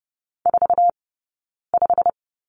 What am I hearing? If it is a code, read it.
Morse code, heard as a keyed tone, about 30 wpm, spaced as for 9 wpm, 690 Hz. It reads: 45
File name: Lesson 18d - Letter m - Two Characters 30wpm